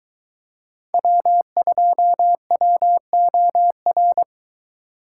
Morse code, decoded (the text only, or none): W2WOR